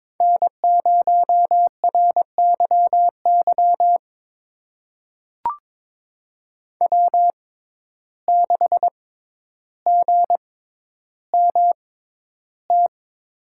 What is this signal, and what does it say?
Morse code, heard as a keyed tone, about 22 wpm, spaced as for 11 wpm, 700 Hz. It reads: N0RYY W6GMT
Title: USA Call Signs - 5 Characters - 6x Character Spacing - 22wpm